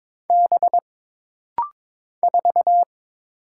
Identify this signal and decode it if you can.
Morse code, heard as a keyed tone, about 22 wpm, 700 Hz. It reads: B 4
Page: This practice set is composed of Letters, Numbers, common punctuations, and common prosigns.